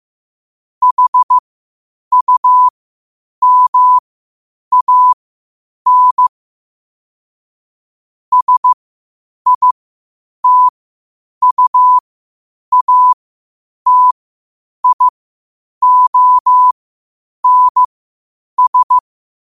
Morse code, heard as a keyed tone, about 15 wpm, spaced as for 8 wpm, 1000 Hz. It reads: HUMAN SITUATIONS
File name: qotdfiles / 8wpm